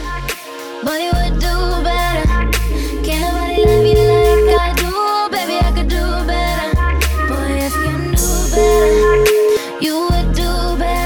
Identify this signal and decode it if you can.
Ringback tone (Europe, 425 Hz), with noise.